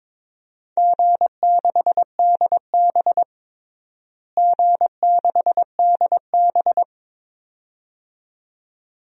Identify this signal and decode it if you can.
Morse code, heard as a keyed tone, about 22 wpm, 700 Hz. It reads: G6DB G6DB
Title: DXCC-100 Call Signs - 4 Characters - Repeated 2x - 22wpm